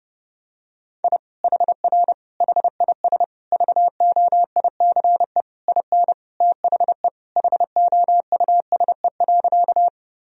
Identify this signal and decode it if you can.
Morse code, heard as a keyed tone, about 30 wpm, 700 Hz. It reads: I HR HIS VOICE IN THE HOUSE.
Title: Sentences from Top 500 Words and Abbreviations 30wpm